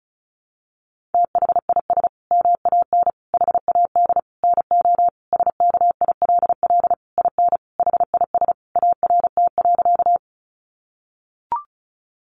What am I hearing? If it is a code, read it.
Morse code, heard as a keyed tone, about 34 wpm, 700 Hz. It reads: THIS MAN HAD NO SKILL IN HIS ART.
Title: Sentences from Top 1200 Words 35wpm